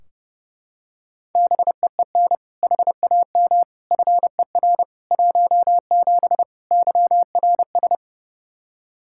Morse code, heard as a keyed tone, about 32 wpm, 700 Hz. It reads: BEEN HAM FER 17 YRS
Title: Practice QSO - Exchange 3 - More Info - 30wpm